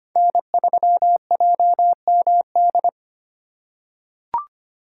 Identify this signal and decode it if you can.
Morse code, heard as a keyed tone, about 25 wpm, 700 Hz. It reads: N3JMD
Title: USA Call Signs - 5 Characters - 8x Character Spacing - 25wpm